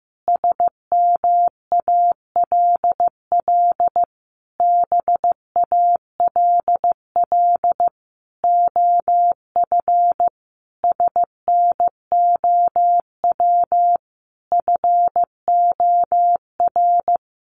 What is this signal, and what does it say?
Morse code, heard as a keyed tone, about 15 wpm, 700 Hz. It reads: SMALL BALL OF SNOW FOR
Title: Easy Phrases 15wpm